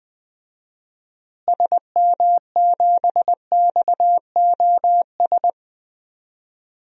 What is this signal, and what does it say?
Morse code, heard as a keyed tone, about 20 wpm, 700 Hz. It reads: SM7XOS
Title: DXCC-100 Call Signs - 6 Characters - Repeated 2x - 20wpm